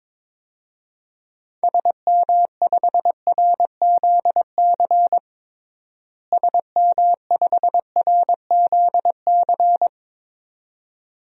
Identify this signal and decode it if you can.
Morse code, heard as a keyed tone, about 22 wpm, 700 Hz. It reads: SM5RZC SM5RZC
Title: DXCC-100 Call Signs - 6 Characters - Repeated 2x - 22wpm